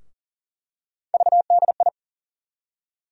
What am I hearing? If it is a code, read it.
Morse code, heard as a keyed tone, about 40 wpm, 700 Hz. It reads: VDI
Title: USA Call Sign Postfixes - 3 Letters 40wpm